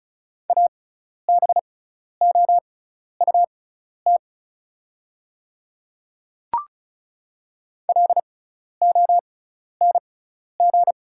Morse code, heard as a keyed tone, about 34 wpm, spaced as for 12 wpm, 700 Hz. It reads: ABOUT LONG
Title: 6x Character Spacing - Top 100 Words - 35wpm